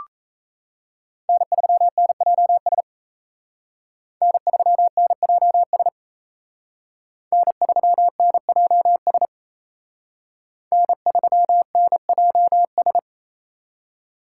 Morse code, heard as a keyed tone, about 32 wpm, 700 Hz. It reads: N3NJS N3NJS N3NJS N3NJS
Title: Speed Racing - USA 1x3 Call Signs 28wpm